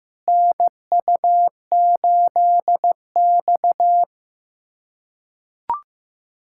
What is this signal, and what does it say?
Morse code, heard as a keyed tone, about 16 wpm, 700 Hz. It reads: NU8X
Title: USA Callsign DE Callsign - 4 Characters - 15wpm